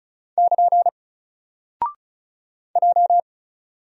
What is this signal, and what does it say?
Morse code, heard as a keyed tone, about 35 wpm, 700 Hz. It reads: <KN> J
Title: Single Character (Letter:Number:Punctuation:Prosign) 35wpm